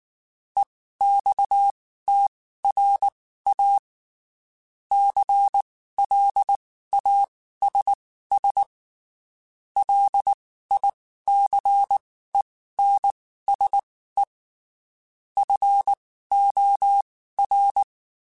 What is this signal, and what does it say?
Morse code, heard as a keyed tone, about 19 wpm, spaced as for 14 wpm, 800 Hz. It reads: EXTRA CLASS LICENSE FOR
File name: (spaced) / QSOs Spaced-mp3s / WA7QOX 2 @18 sp